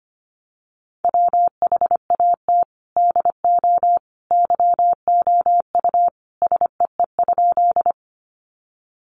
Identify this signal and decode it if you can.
Morse code, heard as a keyed tone, about 25 wpm, 700 Hz. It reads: WHAT DO YOU SEE?
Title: Sentences from Top 100 Words 25wpm